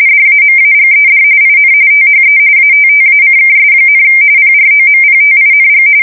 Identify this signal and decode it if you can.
RTTY, about 45.45 baud. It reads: DAMAGED IN THE ATTACKS, PROMPTING